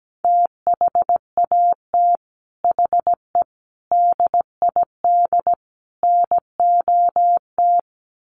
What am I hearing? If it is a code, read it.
Morse code, heard as a keyed tone, about 17 wpm, 700 Hz. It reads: THAT HE DID NOT
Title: Sets of 4 Words 17wpm